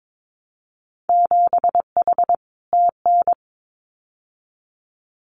Morse code, heard as a keyed tone, about 22 wpm, 700 Hz. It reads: 7H TN
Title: Winter Field Day Exchanges Encore 22wpm